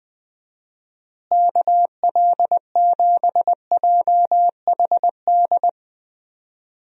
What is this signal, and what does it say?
Morse code, heard as a keyed tone, about 21 wpm, 700 Hz. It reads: KL7JHD
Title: USA Call Signs - 6 Characters - Repeated 3x - 20wpm